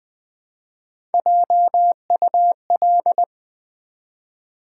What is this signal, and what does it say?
Morse code, heard as a keyed tone, about 20 wpm, 700 Hz. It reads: JUL
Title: Build A Word - 1 Letter At A Time - 4-Letter Words - 20wpm